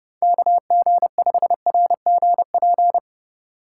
Morse code, heard as a keyed tone, about 30 wpm, 700 Hz. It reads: KG5RGP
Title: Speed Racing - USA 2x3 Call Signs 25wpm